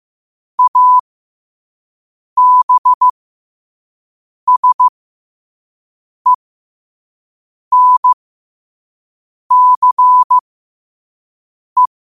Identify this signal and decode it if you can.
Morse code, heard as a keyed tone, about 15 wpm, spaced as for 5 wpm, 1000 Hz. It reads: ABSENCE
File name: qotdfiles / 5wpm